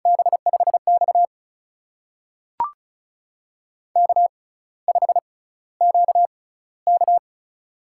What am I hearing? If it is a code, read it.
Morse code, heard as a keyed tone, about 36 wpm, spaced as for 16 wpm, 700 Hz. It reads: B5X K5QK
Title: USA Call Signs - 4 Characters - 6x Character Spacing - 35wpm